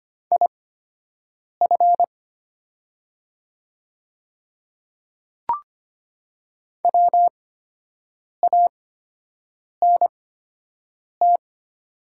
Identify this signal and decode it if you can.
Morse code, heard as a keyed tone, about 25 wpm, spaced as for 7 wpm, 700 Hz. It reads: IF WANT
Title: 8x Character Spacing - Top 300 Words - 25wpm